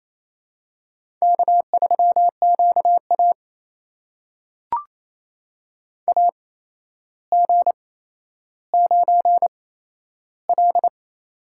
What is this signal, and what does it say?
Morse code, heard as a keyed tone, about 28 wpm, spaced as for 12 wpm, 700 Hz. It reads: K3QA AG9L
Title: USA Call Signs - 8x Character Spacing - 28wpm